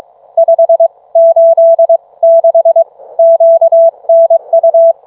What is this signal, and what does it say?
Morse code, heard as a keyed tone, about 22 wpm, spaced as for 19 wpm, 660 Hz. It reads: 586 QNU